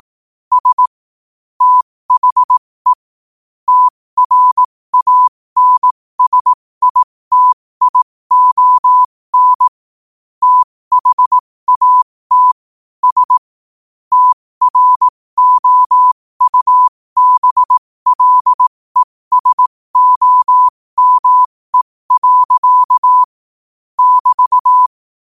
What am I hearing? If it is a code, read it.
Morse code, heard as a keyed tone, about 18 wpm, spaced as for 15 wpm, 1000 Hz. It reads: S THE TRANSITION THAT S TROUBLESOME. <BT>
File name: qotdfiles / 15wpm